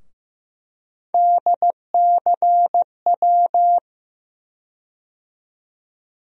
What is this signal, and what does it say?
Morse code, heard as a keyed tone, about 15 wpm, 700 Hz. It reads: DCW
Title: USA Call Sign Postfixes - 3 Letters 15wpm